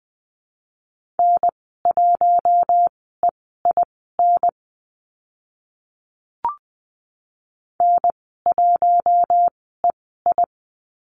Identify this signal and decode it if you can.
Morse code, heard as a keyed tone, about 20 wpm, spaced as for 14 wpm, 700 Hz. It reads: N1EIN N1EI
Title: 042 Lesson 8g - Number 1 - Call Signs 20wpm